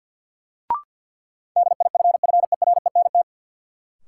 Morse code, heard as a keyed tone, about 48 wpm, 700 Hz. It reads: DIFFERENT
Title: Sentences from Top 600 Words - New Word Drill - 50wpm